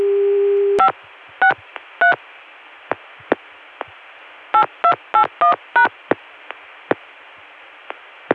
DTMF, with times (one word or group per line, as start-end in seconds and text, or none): 0.8-0.9 5
1.4-2.1 63
4.5-5.9 8281#